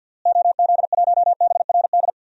Morse code, heard as a keyed tone, about 51 wpm, 700 Hz. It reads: OZ1BRD
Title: DXCC-100 Call Signs - 6 Characters - Repeated 3x - 50wpm